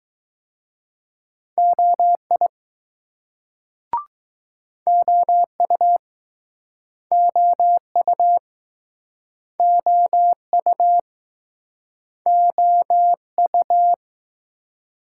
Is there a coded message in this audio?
Yes: Morse code from 1.6 to 13.9 s, heard as a keyed tone, about 20 wpm, 700 Hz. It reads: OI OU OU OU OU